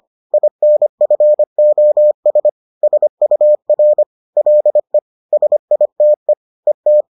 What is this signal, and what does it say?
Morse code, heard as a keyed tone, about 25 wpm, 600 Hz. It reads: INFOS SUR LE SITE ET
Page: bulletin UFT sous forme texte et en CW